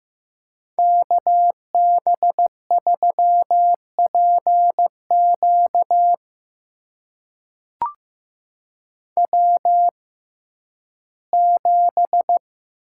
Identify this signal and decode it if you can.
Morse code, heard as a keyed tone, about 15 wpm, spaced as for 11 wpm, 700 Hz. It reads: KB3PQ W7
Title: USA Call Signs - 5 Characters - 6x Character Spacing - 15wpm